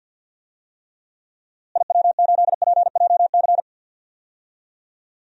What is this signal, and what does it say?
Morse code, heard as a keyed tone, about 49 wpm, 700 Hz. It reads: IW8PJC
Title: DXCC-100 Call Signs - 6 Characters - Repeated 2x - 50wpm